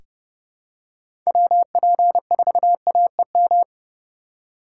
Morse code, heard as a keyed tone, about 30 wpm, 700 Hz. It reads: WP4AEM
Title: USA Call Signs - 6 Characters - 4x Character Spacing - 30wpm